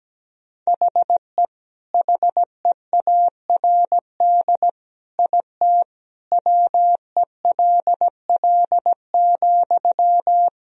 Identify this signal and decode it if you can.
Morse code, heard as a keyed tone, about 17 wpm, 700 Hz. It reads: HE HEARD IT WELL,
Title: Sentences from Top 1000 Words - Review Part 3 - 17wpm